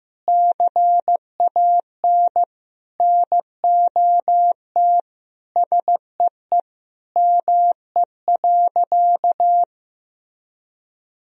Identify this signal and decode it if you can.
Morse code, heard as a keyed tone, about 16 wpm, 700 Hz. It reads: CAN NOT SEE ME.
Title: Sentences from Top 1000 Words - Review Part 3 - 15wpm